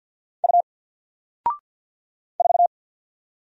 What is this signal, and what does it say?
Morse code, heard as a keyed tone, about 50 wpm, 700 Hz. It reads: U 4
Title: Single Letter-Number Alt 50wpm